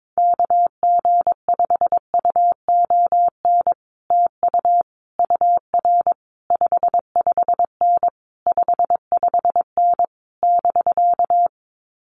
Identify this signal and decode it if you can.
Morse code, heard as a keyed tone, about 22 wpm, 700 Hz. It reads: KG5UON TU UR 55N 55N <BK>
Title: Common QSO Phrases Encore 22wpm